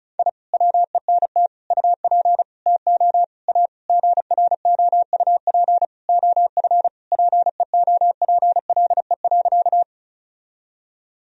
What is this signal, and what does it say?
Morse code, heard as a keyed tone, about 34 wpm, 700 Hz. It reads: I WENT UP TO A GROUP OF PEOPLE.
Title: Sentences from Top 1000 Words - Review Part 1 - 35wpm